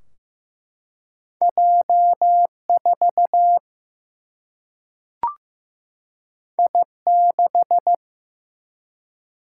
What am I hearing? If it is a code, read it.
Morse code, heard as a keyed tone, about 16 wpm, 700 Hz. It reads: J4 I6
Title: Single-Letter Single-Number 15wpm